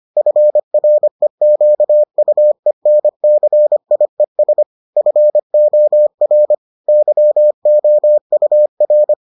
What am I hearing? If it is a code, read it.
Morse code, heard as a keyed tone, about 25 wpm, 600 Hz. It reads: FREQUENCIES FOR YOUR